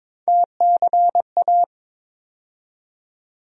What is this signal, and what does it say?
Morse code, heard as a keyed tone, about 22 wpm, 700 Hz. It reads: TCA
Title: USA Call Sign Postfixes - 3 Letters 22wpm